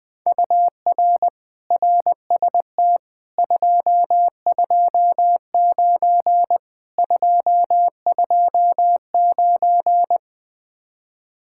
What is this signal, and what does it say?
Morse code, heard as a keyed tone, about 19 wpm, 700 Hz. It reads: UR RST 229 229
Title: Speed-Racing Common QSO Phrases - 17wpm